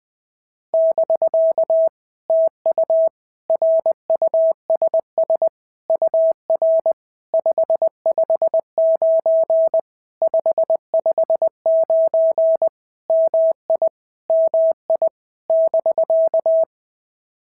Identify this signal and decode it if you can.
Morse code, heard as a keyed tone, about 19 wpm, 650 Hz. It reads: <BK> TU RUSS UR 559 559 MI MI <BK>
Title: POTA QSO Practice - 20wpm